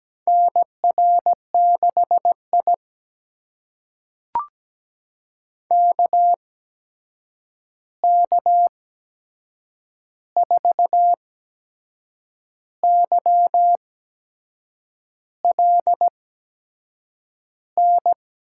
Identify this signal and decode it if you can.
Morse code, heard as a keyed tone, about 17 wpm, spaced as for 6 wpm, 700 Hz. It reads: NR6I KK4YLN
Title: USA Call Signs - 8x Character Spacing - 17wpm